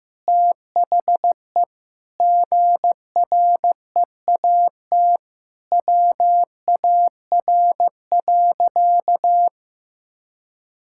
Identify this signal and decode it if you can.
Morse code, heard as a keyed tone, about 15 wpm, 700 Hz. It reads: THE GREAT WAR.
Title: Sentences from Top 1600 Words 15wpm